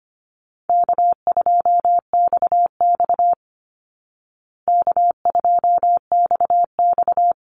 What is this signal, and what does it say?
Morse code, heard as a keyed tone, about 25 wpm, 700 Hz. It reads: K2XX K2XX
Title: Call Signs Repeated 2x - 25wpm